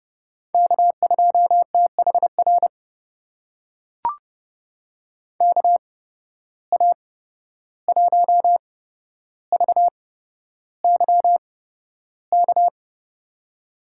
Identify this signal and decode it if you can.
Morse code, heard as a keyed tone, about 30 wpm, spaced as for 12 wpm, 700 Hz. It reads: K2THR KA1VYK
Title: USA Call Signs - 8x Character Spacing - 30wpm